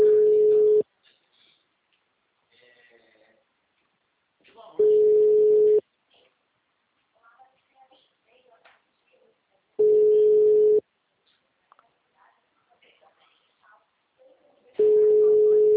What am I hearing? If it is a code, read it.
Ringback tone (Europe, 425 Hz), with speech.